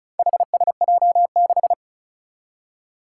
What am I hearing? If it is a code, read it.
Morse code, heard as a keyed tone, about 36 wpm, 700 Hz. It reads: HSJ6
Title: Four Challenging Letter-Number Combinations 35wpm